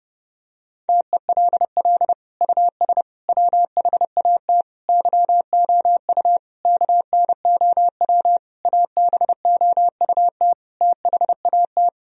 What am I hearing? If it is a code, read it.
Morse code, heard as a keyed tone, about 32 wpm, 700 Hz. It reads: TELL US WHAT YOU KNOW ABOUT THAT
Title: Sentences from Top 1000 Words - Review Part 4 - 30wpm